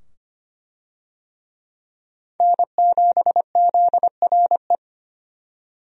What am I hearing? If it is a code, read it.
Morse code, heard as a keyed tone, about 25 wpm, 700 Hz. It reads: N7ZRE
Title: USA Call Signs - 5 Characters - 25wpm